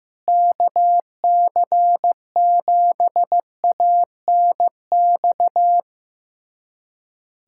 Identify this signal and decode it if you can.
Morse code, heard as a keyed tone, about 15 wpm, 700 Hz. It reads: KC7ANX
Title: USA Callsign DE Callsign 15wpm